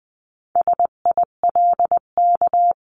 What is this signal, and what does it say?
Morse code, heard as a keyed tone, about 20 wpm, 700 Hz. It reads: SILK
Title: 259 Lesson 42d - Words 18-20wpm